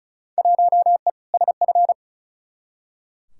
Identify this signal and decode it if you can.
Morse code, heard as a keyed tone, about 36 wpm, 700 Hz. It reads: 1E SF
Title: Field Day Exchanges - 35wpm